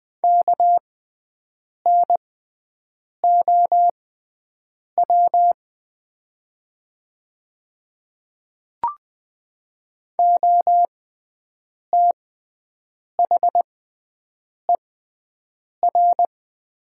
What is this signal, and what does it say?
Morse code, heard as a keyed tone, about 20 wpm, spaced as for 7 wpm, 700 Hz. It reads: KNOW OTHER
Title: 6x Character Spacing - Top 200 Words - 20wpm